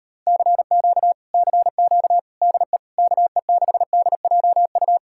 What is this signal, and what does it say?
Morse code, heard as a keyed tone, about 38 wpm, 700 Hz. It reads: CQ CQ DE KE6DJU